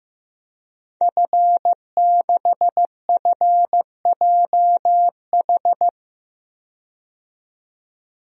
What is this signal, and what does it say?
Morse code, heard as a keyed tone, about 15 wpm, 700 Hz. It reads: F6FJH
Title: DXCC-100 Call Signs - 5 Characters - 15wpm